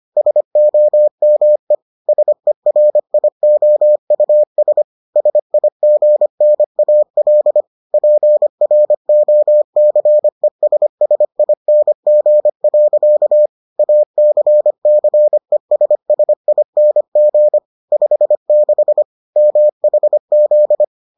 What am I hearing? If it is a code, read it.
Morse code, heard as a keyed tone, about 25 wpm, 600 Hz. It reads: SOME SERIOUS SIGNAL PROCESSING. ACCESSING 56 MHZ